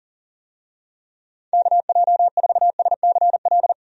Morse code, heard as a keyed tone, about 40 wpm, 700 Hz. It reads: KJ4SCL
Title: USA Call Signs - 6 Characters - 40wpm